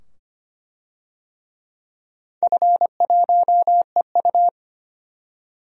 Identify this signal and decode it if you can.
Morse code, heard as a keyed tone, about 25 wpm, 700 Hz. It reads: F1EU
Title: DXCC-100 Call Signs - 4 Characters - 25wpm